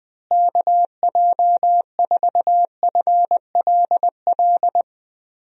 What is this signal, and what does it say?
Morse code, heard as a keyed tone, about 21 wpm, 700 Hz. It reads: KJ4FLL
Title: USA Call Signs - 6 Characters - Repeated 3x - 20wpm